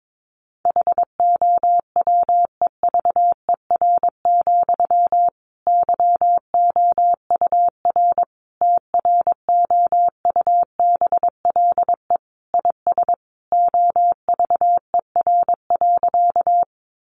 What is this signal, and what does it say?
Morse code, heard as a keyed tone, about 22 wpm, 700 Hz. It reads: HOWEVER, YOUR TROUBLE IS OVER.
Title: Sentences from Top 1200 Words 22wpm